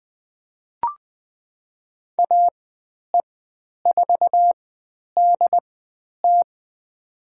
Morse code, heard as a keyed tone, about 20 wpm, spaced as for 10 wpm, 700 Hz. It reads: AE4DT
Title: 095 Lesson 16g - Letter c - Call Signs 20wpm